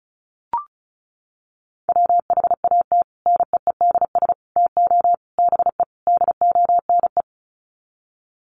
Morse code, heard as a keyed tone, about 34 wpm, 700 Hz. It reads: WHAT NEEDS TO BE DONE
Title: Sets of 5 Words Encore 35wpm